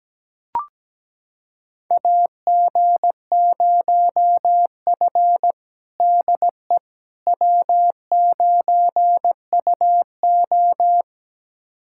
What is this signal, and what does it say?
Morse code, heard as a keyed tone, about 17 wpm, 700 Hz. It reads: AG0F DE W9UO
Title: USA Callsign DE Callsign - 4 Characters - 17wpm